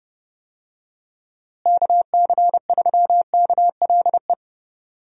Morse code, heard as a keyed tone, about 29 wpm, 700 Hz. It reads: KC3KLE
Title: USA Call Signs - 6 Characters - Repeated 3x - 30wpm